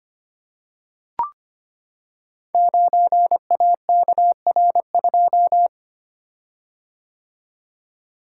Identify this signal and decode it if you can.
Morse code, heard as a keyed tone, about 25 wpm, 700 Hz. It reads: 9AKR2
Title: Code Groups - 5 Letter:Numbers - 25wpm